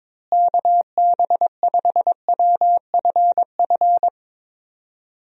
Morse code, heard as a keyed tone, about 22 wpm, 700 Hz. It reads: KB5WFF
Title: USA Call Signs - 6 Characters - Repeated 3x - 22wpm